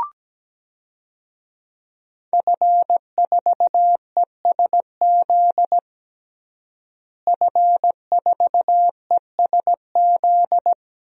Morse code, heard as a keyed tone, about 17 wpm, 700 Hz. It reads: F4ESZ F4ESZ
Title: DXCC-100 Call Signs - 5 Characters - Repeated 2x - 17wpm